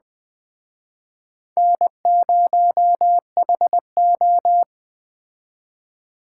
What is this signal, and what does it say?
Morse code, heard as a keyed tone, about 21 wpm, 700 Hz. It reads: N0HO
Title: Speed Racing - USA 1x2 Call Signs 17wpm